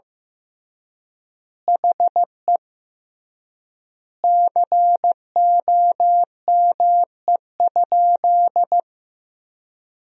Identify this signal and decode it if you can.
Morse code, heard as a keyed tone, about 15 wpm, 700 Hz. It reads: HE COME?
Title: 3x Word-Spacing - Sentences from Top 200 Words - 15wpm